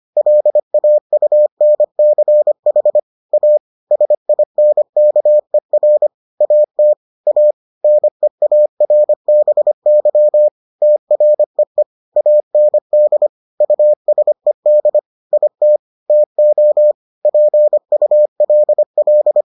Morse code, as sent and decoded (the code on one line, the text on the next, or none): .-.. .- ..- -. -.-. .... / .- / ... .. -. -.- . .-. / .- - / .- / -. .- .-. -... -.-- / - .-. . / .- -. -.. / ..- ... -.. / .. - / - --- / .--. ..- .-.. .-..
LAUNCH A SINKER AT A NEARBY TREE AND USED IT TO PULL